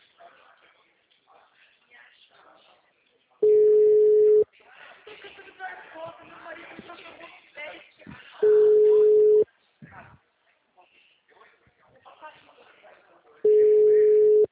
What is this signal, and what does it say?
Ringback tone (Europe, 425 Hz), with speech.